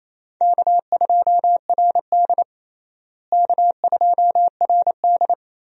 Morse code, heard as a keyed tone, about 28 wpm, 700 Hz. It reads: K2RD K2RD
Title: USA Call Signs - 4 Characters - Repeated 3x - 28wpm